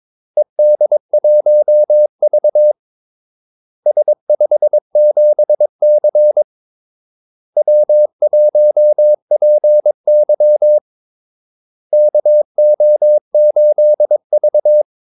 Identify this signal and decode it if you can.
Morse code, heard as a keyed tone, about 22 wpm, 600 Hz. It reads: ED1V S57C W1PY KO8V
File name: Callsigns-22wpm